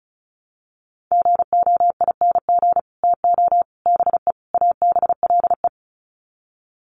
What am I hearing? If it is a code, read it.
Morse code, heard as a keyed tone, about 34 wpm, 700 Hz. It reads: GOING TO BE ABLE
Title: Sets of 4 Words 35wpm